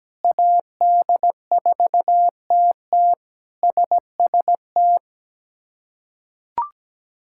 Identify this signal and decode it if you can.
Morse code, heard as a keyed tone, about 17 wpm, 700 Hz. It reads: AD4TT SST
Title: K1USN SST QSO Practice - 17wpm